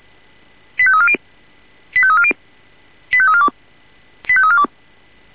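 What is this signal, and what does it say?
Five-tone selective call (ZVEI), heard as quick runs of five tones, tones of about 70 ms. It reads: 85340 85340 85342 85342
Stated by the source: ZVEI five-tone selective call 85342 received on an SDR.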